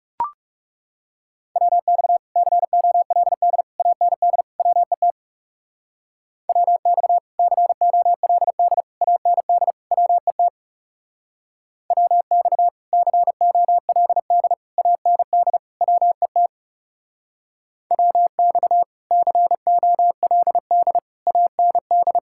Morse code, heard as a keyed tone, about 37 wpm, 700 Hz. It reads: WX COLD AND WET WX COLD AND WET WX COLD AND WET WX COLD AND
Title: Speed-Racing Common QSO Phrases - 30wpm